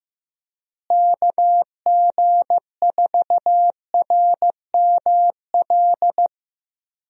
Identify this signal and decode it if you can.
Morse code, heard as a keyed tone, about 15 wpm, 700 Hz. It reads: KG4RML
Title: USA Call Signs - 6 Characters - Repeated 2x - 15wpm